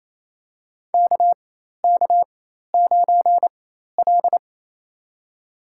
Morse code, heard as a keyed tone, about 27 wpm, spaced as for 13 wpm, 700 Hz. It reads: KK9L
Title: USA Call Signs - 4 Characters - 4x Character Spacing - 28wpm